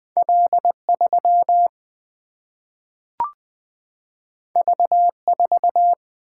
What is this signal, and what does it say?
Morse code, heard as a keyed tone, about 20 wpm, 700 Hz. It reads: L3 V4
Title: Single-Letter Single-Number 20wpm